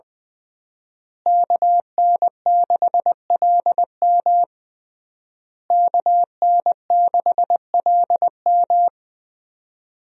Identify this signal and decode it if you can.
Morse code, heard as a keyed tone, about 20 wpm, 700 Hz. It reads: KN6LM KN6LM
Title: USA Call Signs - 5 Characters - Repeated 3x - 20wpm